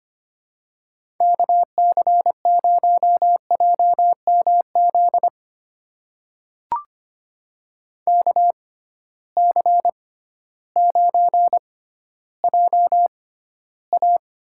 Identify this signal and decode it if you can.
Morse code, heard as a keyed tone, about 25 wpm, spaced as for 13 wpm, 700 Hz. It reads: KC0JMZ KC9JA
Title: USA Call Signs - 6 Characters - 6x Character Spacing - 25wpm